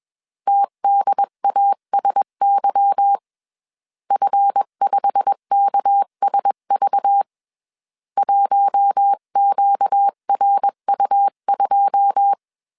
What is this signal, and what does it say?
Morse code, heard as a keyed tone, about 21 wpm, 800 Hz. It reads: TDASY F5KSV 1QRU2